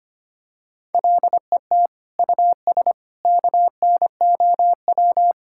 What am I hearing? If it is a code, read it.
Morse code, heard as a keyed tone, about 25 wpm, 700 Hz. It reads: LET US KNOW
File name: Sets of 9 Words - 25wpm